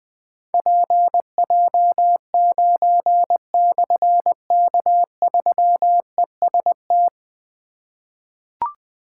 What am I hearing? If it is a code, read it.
Morse code, heard as a keyed tone, about 20 wpm, 700 Hz. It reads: PJ9/K3EST
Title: Difficult Call Signs 20wpm